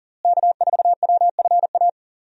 Morse code, heard as a keyed tone, about 40 wpm, 700 Hz. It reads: K4WFA
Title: USA Call Signs - 5 Characters - Repeated 2x - 40wpm